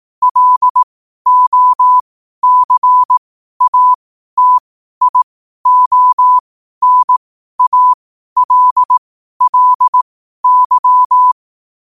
Morse code, heard as a keyed tone, about 18 wpm, spaced as for 12 wpm, 1000 Hz. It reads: LOCATIONALLY
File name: qotdfiles / 12wpm